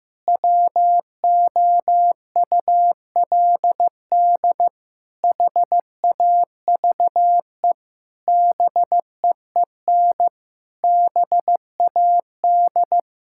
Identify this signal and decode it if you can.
Morse code, heard as a keyed tone, about 15 wpm, 700 Hz. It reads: WOULD HAVE BEEN BAD
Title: Sentences from Top 1000 Words - Review Part 5 - 15wpm